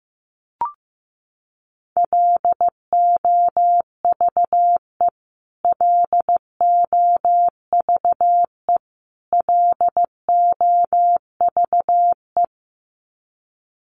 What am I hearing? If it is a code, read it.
Morse code, heard as a keyed tone, about 15 wpm, 700 Hz. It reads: LOVE LOVE LOVE
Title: Top 500 Words Repeated 15wpm